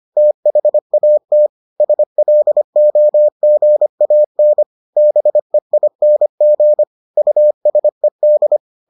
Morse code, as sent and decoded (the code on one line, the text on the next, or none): - .... .- - / ... .-.. --- --. .- -. / -... .. -. --. / ..- ... -..
THAT SLOGAN BEING USED